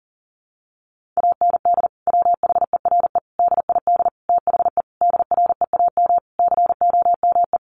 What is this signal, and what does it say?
Morse code, heard as a keyed tone, about 40 wpm, 700 Hz. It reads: AND WHERE DID THE DREAM COME
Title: Sentences from Top 1500 Words 40wpm